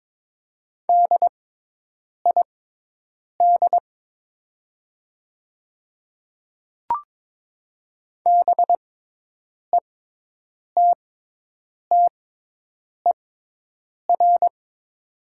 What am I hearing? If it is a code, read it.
Morse code, heard as a keyed tone, about 22 wpm, spaced as for 8 wpm, 700 Hz. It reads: DID BETTER
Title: 6x Character Spacing - Top 500 Words - 22wpm